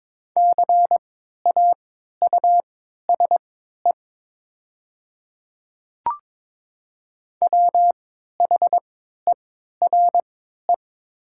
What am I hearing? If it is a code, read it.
Morse code, heard as a keyed tone, about 21 wpm, spaced as for 12 wpm, 700 Hz. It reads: CAUSE WHERE
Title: 3x Character Spacing - Top 300 Words - 22wpm